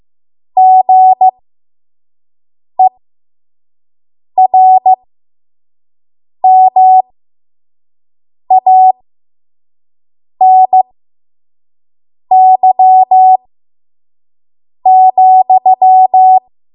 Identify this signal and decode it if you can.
Morse code, heard as a keyed tone, about 15 wpm, spaced as for 5 wpm, 750 Hz. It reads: GERMANY,
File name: Morse / 05WPM